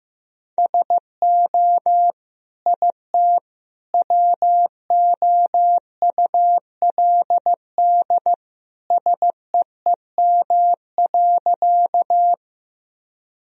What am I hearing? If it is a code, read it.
Morse code, heard as a keyed tone, about 15 wpm, 700 Hz. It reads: SO IT WOULD SEEM.
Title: Sentences from Top 1000 Words - Review Part 1 - 15wpm